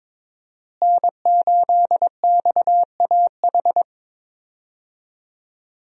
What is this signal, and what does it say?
Morse code, heard as a keyed tone, about 22 wpm, 700 Hz. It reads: N8XAH
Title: USA Call Signs - 5 Characters - Repeated 3x - 22wpm